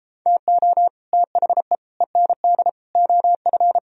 Morse code, heard as a keyed tone, about 33 wpm, 700 Hz. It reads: TO THE END OF